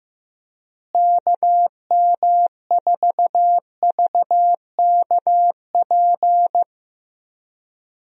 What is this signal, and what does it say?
Morse code, heard as a keyed tone, about 15 wpm, 700 Hz. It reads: KM4VKP
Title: USA Call Signs - 6 Characters - Repeated 2x - 15wpm